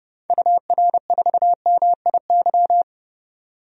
Morse code, heard as a keyed tone, about 30 wpm, 700 Hz. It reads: UR4MIY
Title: DXCC-100 Call Signs - 6 Characters - 30wpm